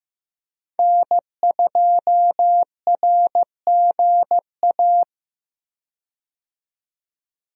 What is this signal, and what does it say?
Morse code, heard as a keyed tone, about 15 wpm, 700 Hz. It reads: N2RGA
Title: USA Call Signs - 5 Characters - 15wpm